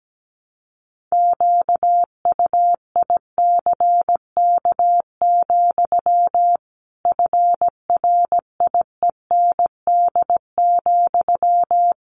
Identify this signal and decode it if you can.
Morse code, heard as a keyed tone, about 17 wpm, 700 Hz. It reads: QUICK, FRIEND,
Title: Sentences from Top 500 Words 17wpm